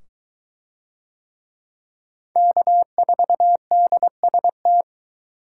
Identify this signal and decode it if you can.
Morse code, heard as a keyed tone, about 20 wpm, 700 Hz. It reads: K4DST